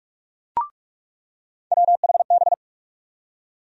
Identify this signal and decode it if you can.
Morse code, heard as a keyed tone, about 44 wpm, 700 Hz. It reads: WHB
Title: USA Call Sign Postfixes - 3 Letters 45wpm